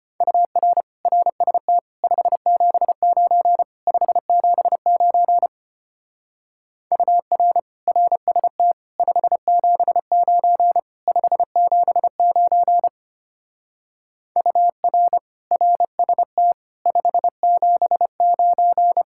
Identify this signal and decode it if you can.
Morse code, heard as a keyed tone, about 30 wpm, 700 Hz. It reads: UR RST 579 579 UR RST 579 579 UR RST 579
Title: Speed-Racing Common QSO Phrases - 25wpm